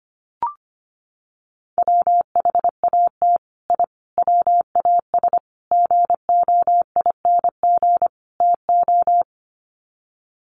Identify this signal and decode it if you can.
Morse code, heard as a keyed tone, about 25 wpm, 700 Hz. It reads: WHAT I WAS GOING TO
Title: Sets of 5 Words Encore2 25wpm